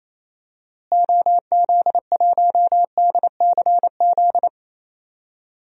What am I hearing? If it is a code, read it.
Morse code, heard as a keyed tone, about 28 wpm, 700 Hz. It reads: OZ1DCZ
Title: DXCC-100 Call Signs - 6 Characters - Repeated 2x - 28wpm